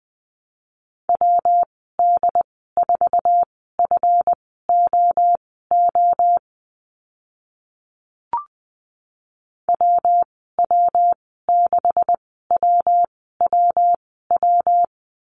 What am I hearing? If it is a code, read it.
Morse code, heard as a keyed tone, about 20 wpm, spaced as for 15 wpm, 700 Hz. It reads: WD4FOO WW6WWW
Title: 140 Lesson 23g - Letter f - Call Signs 20wpm